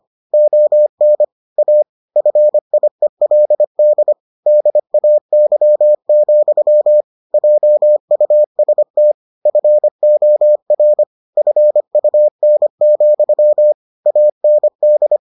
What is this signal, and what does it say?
Morse code, heard as a keyed tone, about 25 wpm, 600 Hz. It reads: ON A FIELD DAY, JUST FOR FUN, AND